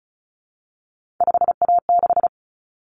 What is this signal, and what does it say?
Morse code, heard as a keyed tone, about 34 wpm, 700 Hz. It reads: 5A6
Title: Common Cut Numbers - 3-Digit 35wpm